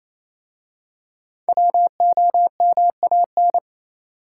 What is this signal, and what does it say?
Morse code, heard as a keyed tone, about 28 wpm, 700 Hz. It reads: WOMAN